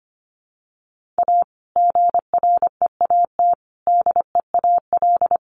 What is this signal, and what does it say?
Morse code, heard as a keyed tone, about 25 wpm, 700 Hz. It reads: A GREAT DEAL
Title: Sets of 3 Words 25wpm